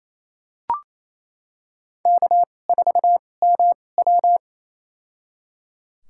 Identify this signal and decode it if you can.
Morse code, heard as a keyed tone, about 28 wpm, spaced as for 20 wpm, 700 Hz. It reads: K4MW
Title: USA Call Signs - 2x Character Spacing - 28wpm